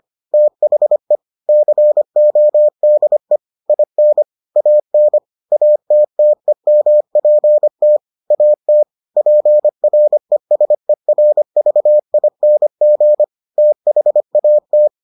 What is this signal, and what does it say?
Morse code, heard as a keyed tone, about 25 wpm, 600 Hz. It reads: THE CODE IN AN ATTEMPT AT PRESERVING THAT